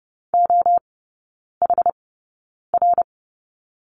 Morse code, heard as a keyed tone, about 30 wpm, spaced as for 9 wpm, 700 Hz. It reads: OHR
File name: Lesson 17e - Letter u - Three Characters 30wpm